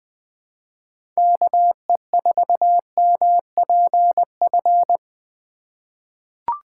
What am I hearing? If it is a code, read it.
Morse code, heard as a keyed tone, about 21 wpm, 700 Hz. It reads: KE4MPF
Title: USA Call Signs - 6x Character Spacing - 20wpm